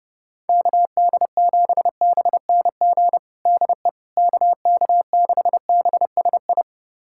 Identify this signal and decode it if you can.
Morse code, heard as a keyed tone, about 32 wpm, 700 Hz. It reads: KD7BNG DE KK6BSI
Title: USA Callsign DE Callsign - 6 Characters - 30wpm